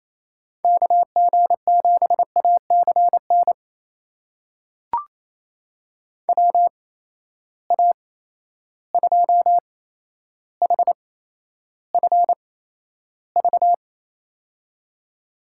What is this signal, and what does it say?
Morse code, heard as a keyed tone, about 27 wpm, spaced as for 12 wpm, 700 Hz. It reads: KG7ACN WA2HFV
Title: USA Call Signs - 6 Characters - 8x Character Spacing - 28wpm